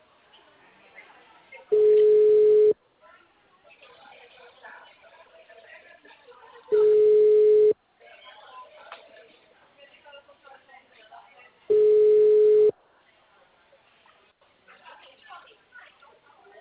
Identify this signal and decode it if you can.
Ringback tone (Europe, 425 Hz), with speech.